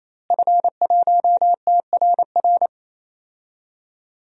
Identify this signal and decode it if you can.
Morse code, heard as a keyed tone, about 28 wpm, 700 Hz. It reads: F1TRR
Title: DXCC-100 Call Signs - 5 Characters - Repeated 3x - 28wpm